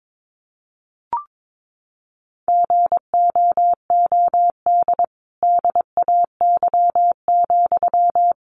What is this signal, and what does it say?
Morse code, heard as a keyed tone, about 22 wpm, 700 Hz. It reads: GOOD DAY,
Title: Sentences from Top 1400 Words 22wpm